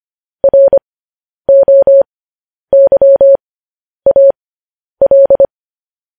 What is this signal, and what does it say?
Morse code, heard as a keyed tone, about 25 wpm, spaced as for 10 wpm, 550 Hz. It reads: ROYAL
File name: podcast / News headlines-1x-25x10